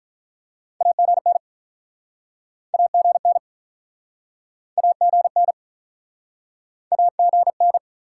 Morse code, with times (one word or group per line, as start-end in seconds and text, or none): 0.8-1.4 AGN
2.7-3.4 AGN
4.8-5.5 AGN
6.9-7.8 AGN